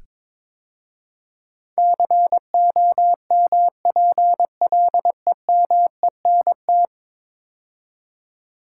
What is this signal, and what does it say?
Morse code, heard as a keyed tone, about 23 wpm, 700 Hz. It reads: COMPLEMENT
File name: practice / 22wpm / 10 Letter Words - First 3 Letters Given - 22wpm